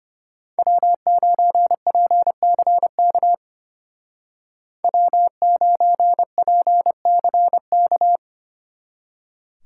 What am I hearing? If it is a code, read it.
Morse code, heard as a keyed tone, about 32 wpm, 700 Hz. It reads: W9PCK W9PCK